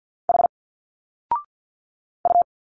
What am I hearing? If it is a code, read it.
Morse code, heard as a keyed tone, about 50 wpm, 700 Hz. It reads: H U
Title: Single Letters 50wpm